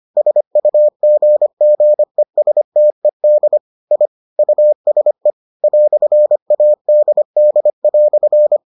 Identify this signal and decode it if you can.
Morse code, heard as a keyed tone, about 25 wpm, 600 Hz. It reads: SUGGESTED I USE "ADD"